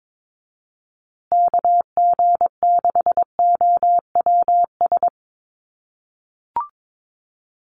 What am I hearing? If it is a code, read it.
Morse code, heard as a keyed tone, about 22 wpm, 700 Hz. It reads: KG6OWS
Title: USA Call Signs Repeated 3x 22wpm